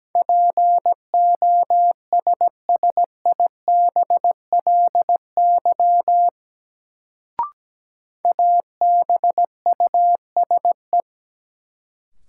Morse code, heard as a keyed tone, about 17 wpm, 700 Hz. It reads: POSSIBLY ABUSE
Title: Sentences from Top 1300 Words - New Word Drill Alt - 17wpm